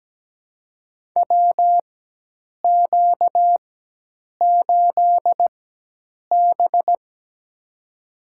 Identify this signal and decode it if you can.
Morse code, heard as a keyed tone, about 17 wpm, spaced as for 8 wpm, 700 Hz. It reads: WQ8B
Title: USA Call Signs - 4x Character Spacing - 17wpm